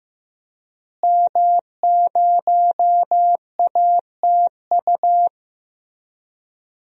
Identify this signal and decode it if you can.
Morse code, heard as a keyed tone, about 15 wpm, 700 Hz. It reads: M0ATU
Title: DXCC-100 Call Signs - 5 Characters - Repeated 3x - 15wpm